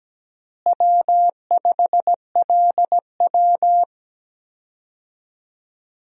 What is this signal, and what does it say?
Morse code, heard as a keyed tone, about 17 wpm, 700 Hz. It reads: W5LW
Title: USA Call Signs - 4 Characters - Repeated 3x - 17wpm